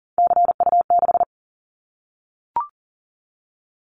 Morse code, heard as a keyed tone, about 39 wpm, 700 Hz. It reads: CU6
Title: Callsign Prefixes for Top 50 DXCC - 40wpm